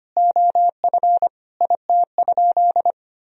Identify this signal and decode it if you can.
Morse code, heard as a keyed tone, about 25 wpm, 700 Hz. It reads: OF IT?
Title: Sentences from Top 100 Words and Abbreviations 25wpm